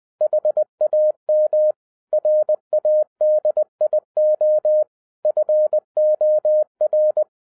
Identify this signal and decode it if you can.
Morse code, heard as a keyed tone, about 20 wpm, 610 Hz. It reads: HAM RADIO FOR
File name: 2011-10-mp3 / n13-TIPS FROM MY 34 YEARS OF EXPERIENCE 20wpm